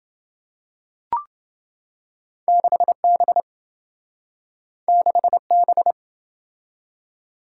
Morse code, heard as a keyed tone, about 27 wpm, 700 Hz. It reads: 6B 6B